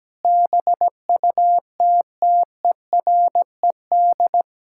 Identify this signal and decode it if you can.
Morse code, heard as a keyed tone, about 17 wpm, 700 Hz. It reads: BUTTERED
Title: Pangram Challenge v2 - 17wpm